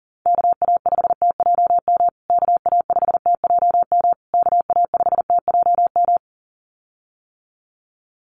Morse code, heard as a keyed tone, about 40 wpm, 700 Hz. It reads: KA5TJM KA5TJM KA5TJM
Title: USA Call Signs Repeated 3x 40wpm